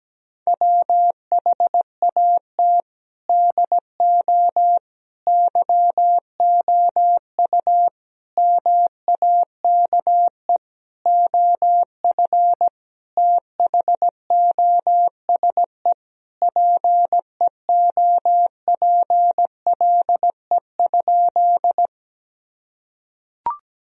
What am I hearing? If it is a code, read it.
Morse code, heard as a keyed tone, about 17 wpm, 700 Hz. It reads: WHAT DO YOU MAKE OF THOSE PEOPLE?
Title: Sentences from Top 1000 Words - Review Part 1 - 17wpm